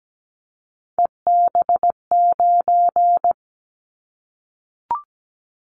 Morse code, heard as a keyed tone, about 17 wpm, 700 Hz. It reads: EB9
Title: Callsign Prefixes for Top 100 DXCC - 17wpm